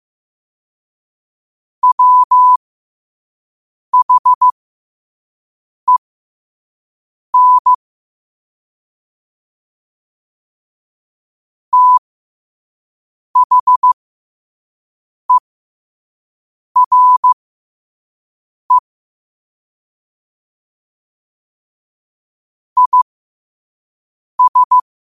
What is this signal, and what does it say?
Morse code, heard as a keyed tone, about 15 wpm, spaced as for 5 wpm, 1000 Hz. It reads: WHEN THERE IS
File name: qotdfiles / 5wpm